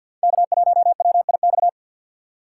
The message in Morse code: K1WIX

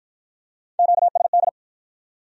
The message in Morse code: QSD

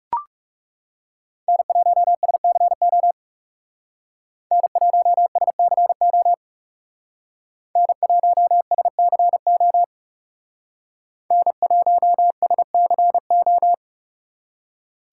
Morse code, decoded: N1SCO N1SCO N1SCO N1SCO

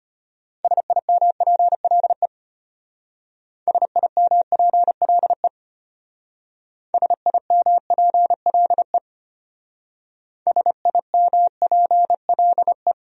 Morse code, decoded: SIMPLE SIMPLE SIMPLE SIMPLE

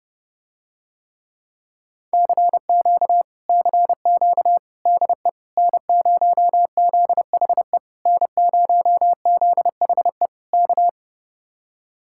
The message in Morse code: CQ CQ DE N0ZHE N0ZHE K

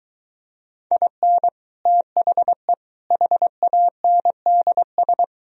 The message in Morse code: IN THE HANDS

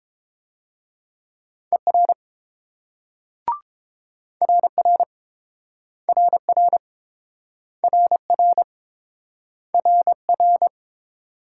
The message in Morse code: ER RR RR RR RR

